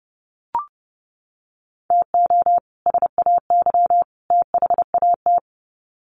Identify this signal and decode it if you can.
Morse code, heard as a keyed tone, about 30 wpm, 700 Hz. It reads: TO SAY THAT
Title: Sets of 3 Words 30wpm